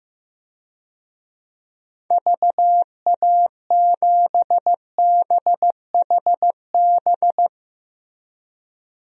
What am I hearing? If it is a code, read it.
Morse code, heard as a keyed tone, about 15 wpm, 700 Hz. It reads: VA7BHB
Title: DXCC-100 Call Signs - 6 Characters - Repeated 3x - 15wpm